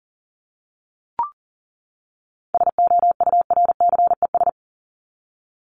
Morse code, heard as a keyed tone, about 40 wpm, 700 Hz. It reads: SOURCES
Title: ES:S Suffix - Top 100 Words with Base Words 40wpm